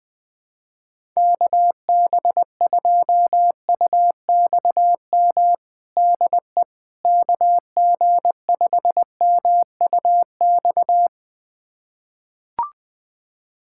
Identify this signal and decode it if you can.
Morse code, heard as a keyed tone, about 20 wpm, 700 Hz. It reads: KB2UXM DE KG5MUX
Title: USA Callsign DE Callsign - 6 Characters - 20wpm